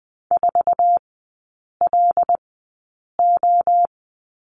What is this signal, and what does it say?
Morse code, heard as a keyed tone, about 20 wpm, spaced as for 8 wpm, 700 Hz. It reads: 4LO
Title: 074 Lesson 13e - Letter L - Three Characters 20wpm